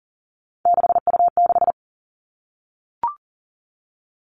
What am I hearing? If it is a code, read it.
Morse code, heard as a keyed tone, about 40 wpm, 700 Hz. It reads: 6U6
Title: All Cut Numbers - 3-Digit 40wpm